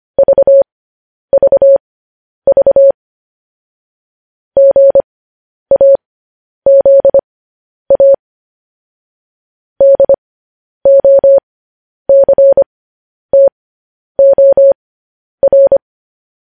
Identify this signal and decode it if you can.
Morse code, heard as a keyed tone, about 25 wpm, spaced as for 10 wpm, 550 Hz. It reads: VVV GAZA DOCTOR
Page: news headlines in morse code at 10 wpm